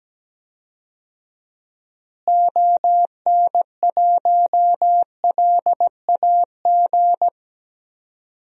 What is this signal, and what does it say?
Morse code, heard as a keyed tone, about 17 wpm, 700 Hz. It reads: ON1LAG
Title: DXCC-100 Call Signs - 6 Characters - Repeated 3x - 17wpm